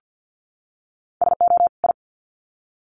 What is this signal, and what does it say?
Morse code, heard as a keyed tone, about 50 wpm, 700 Hz. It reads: SO I
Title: Sets of 2 Words 50wpm